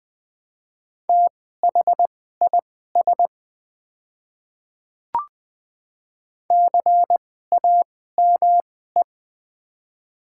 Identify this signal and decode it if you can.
Morse code, heard as a keyed tone, about 20 wpm, spaced as for 14 wpm, 690 Hz. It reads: THIS CAME
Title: 2x Character Spacing - Top 300 Words - 20wpm